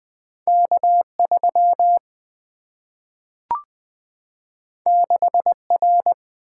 Morse code, heard as a keyed tone, about 20 wpm, 700 Hz. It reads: K3 6R
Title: Single-Letter Single-Number 20wpm